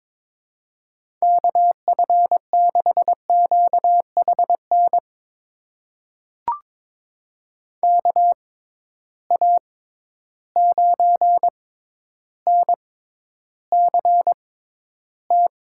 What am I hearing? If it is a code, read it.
Morse code, heard as a keyed tone, about 23 wpm, spaced as for 11 wpm, 700 Hz. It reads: KF6QHN KA9NCT
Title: USA Call Signs - 6 Characters - 6x Character Spacing - 22wpm